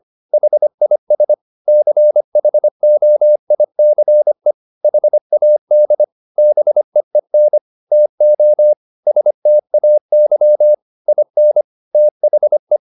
Morse code, heard as a keyed tone, about 25 wpm, 600 Hz. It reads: HIS CHOICE HAD BEEN TO STAY IN THE
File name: the old man and the sea part 1 25wpm